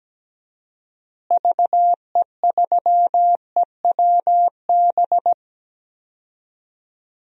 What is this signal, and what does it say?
Morse code, heard as a keyed tone, about 17 wpm, 700 Hz. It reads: VE3EWB
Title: DXCC-100 Call Signs - 6 Characters - Repeated 2x - 17wpm